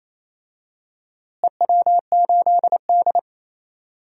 Morse code, heard as a keyed tone, about 28 wpm, 700 Hz. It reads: EW8D
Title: DXCC-100 Call Signs - 4 Characters - Repeated 2x - 28wpm